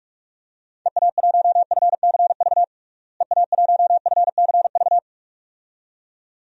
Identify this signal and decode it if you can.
Morse code, heard as a keyed tone, about 46 wpm, 700 Hz. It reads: EA1FCV EA1FCV
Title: DXCC-100 Call Signs - 6 Characters - Repeated 2x - 45wpm